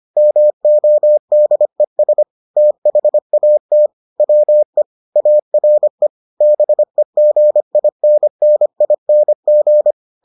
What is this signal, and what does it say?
Morse code, heard as a keyed tone, about 25 wpm, 600 Hz. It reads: MODES THAT WE ARE BEGINNING